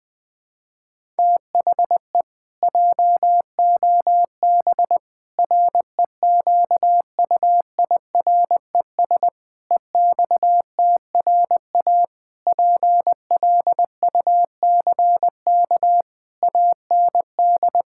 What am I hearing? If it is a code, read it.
Morse code, heard as a keyed tone, about 21 wpm, 700 Hz. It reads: THE JOB REQUIRES EXTRA PLUCK AND
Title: Pangram Challenge v2 - 20wpm